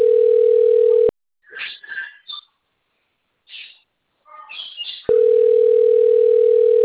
Ringback tone (North America), 440 + 480 Hz.
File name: rg-1600-unknown- - - .60812